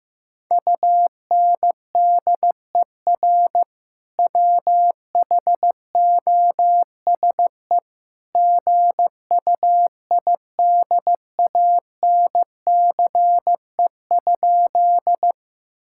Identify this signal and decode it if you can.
Morse code, heard as a keyed tone, about 15 wpm, 700 Hz. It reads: UNDER WHOSE GUIDANCE?
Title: Sentences from Top 1600 Words 15wpm